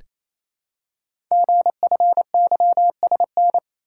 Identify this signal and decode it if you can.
Morse code, heard as a keyed tone, about 28 wpm, 700 Hz. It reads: GFYSN
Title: Code Groups - 5 Letter:Numbers - 28wpm